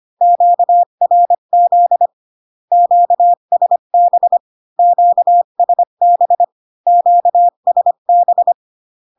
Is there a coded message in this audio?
Yes: Morse code from 0.2 to 8.5 s, heard as a keyed tone, about 25 wpm, 700 Hz. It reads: QRZ QSB QSB QSB